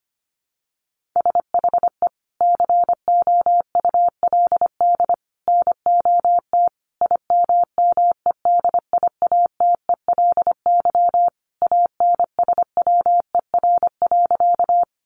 Morse code, heard as a keyed tone, about 25 wpm, 700 Hz. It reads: SHE COULD NOT IMMEDIATELY ANSWER.
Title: Sentences from Top 1100 Words 25wpm